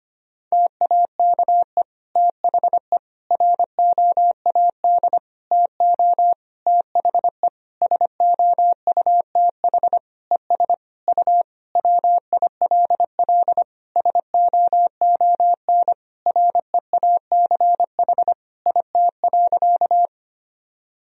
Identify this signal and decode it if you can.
Morse code, heard as a keyed tone, about 25 wpm, 700 Hz. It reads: TAKE THE ROAD TO THE SOUTH ES U WILL SOON REACH IT.
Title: Sentences from Top 400 Words and Abbreviations 25wpm